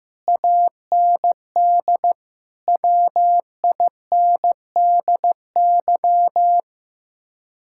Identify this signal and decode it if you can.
Morse code, heard as a keyed tone, about 15 wpm, 700 Hz. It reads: AND WINDY
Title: Speed-Racing Common QSO Phrases Encore - 15wpm